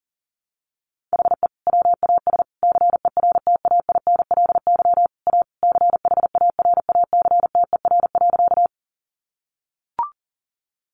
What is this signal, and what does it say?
Morse code, heard as a keyed tone, about 40 wpm, 700 Hz. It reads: HE WAS CERTAINLY A CHARACTER.
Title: Sentences from Top 1200 Words 40wpm